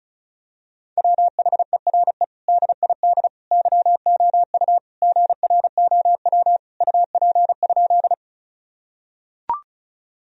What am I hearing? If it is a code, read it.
Morse code, heard as a keyed tone, about 34 wpm, 700 Hz. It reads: WHERE DID YOU GROW UP?
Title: Sentences from Top 1000 Words - Review Part 1 - 35wpm